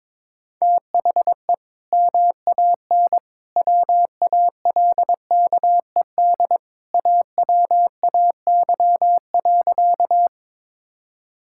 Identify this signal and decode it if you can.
Morse code, heard as a keyed tone, about 22 wpm, 700 Hz. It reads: THE MAN WALKED AWAY.
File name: practice / 22wpm / Sentences with 2-Word Phrases - 22wpm